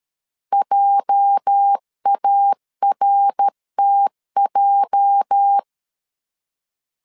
Morse code, heard as a keyed tone, about 13 wpm, 800 Hz. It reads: JARTJ
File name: LEZIONI / 8 YJ45 / 8 YJ 12wpm 2235